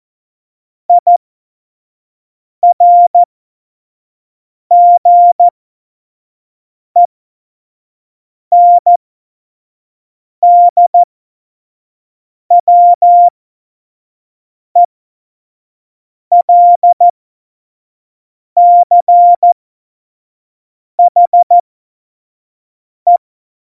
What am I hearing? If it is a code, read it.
Morse code, heard as a keyed tone, about 14 wpm, spaced as for 5 wpm, 700 Hz. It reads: IRGENDWELCHE
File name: Morsekurs / Witze / Witz 20 March 2018, 25 F70